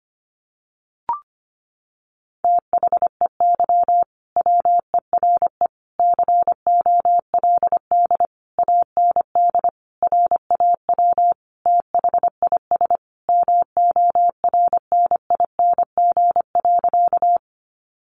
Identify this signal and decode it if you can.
Morse code, heard as a keyed tone, about 25 wpm, 700 Hz. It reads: THEY WERE COLD AND RAW THIS MORNING.
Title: Sentences from Top 1500 Words 25wpm